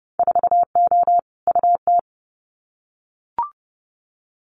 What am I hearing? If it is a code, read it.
Morse code, heard as a keyed tone, about 30 wpm, 700 Hz. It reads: 4O UT
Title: Winter Field Day Exchanges Encore 30wpm